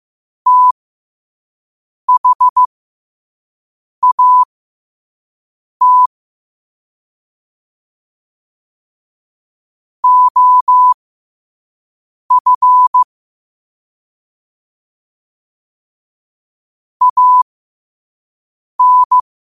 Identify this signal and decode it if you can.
Morse code, heard as a keyed tone, about 15 wpm, spaced as for 5 wpm, 1000 Hz. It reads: THAT OF AN
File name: qotdfiles / 5wpm